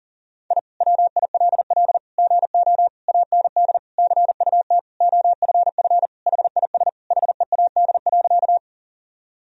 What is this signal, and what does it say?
Morse code, heard as a keyed tone, about 40 wpm, 700 Hz. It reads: I WILL GO AND CUT OFF HIS HEAD.
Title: Sentences from Top 1000 Words - Review Part 1 - 40wpm